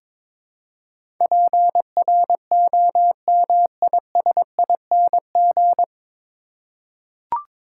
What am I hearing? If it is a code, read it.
Morse code, heard as a keyed tone, about 22 wpm, 700 Hz. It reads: PROMISING
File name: 9 Letter Words - First 3 Letters Given - 22wpm